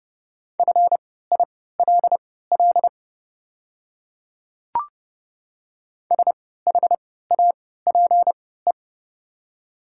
Morse code, heard as a keyed tone, about 30 wpm, spaced as for 17 wpm, 690 Hz. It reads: FILL SHAPE
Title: 3x Character Spacing - Top 500 Words - 30wpm